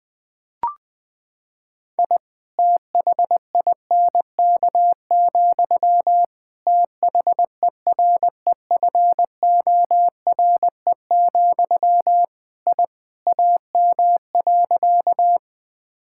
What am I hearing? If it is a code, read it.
Morse code, heard as a keyed tone, about 21 wpm, 700 Hz. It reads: I THINK, THEREFORE, I AM.
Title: Sentences from Top 1000 Words - Review Part 5 - 20wpm